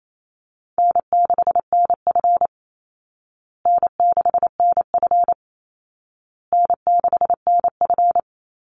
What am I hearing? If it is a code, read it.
Morse code, heard as a keyed tone, about 28 wpm, 700 Hz. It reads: N6NF N6NF N6NF
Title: Call Signs Repeated 2x - 28wpm